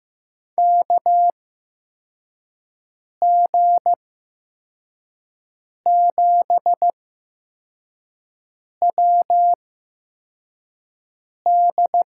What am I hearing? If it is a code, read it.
Morse code, heard as a keyed tone, about 15 wpm, spaced as for 4 wpm, 700 Hz. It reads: KG7WD
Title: USA Call Signs - 6 Characters - 8x Character Spacing - 15wpm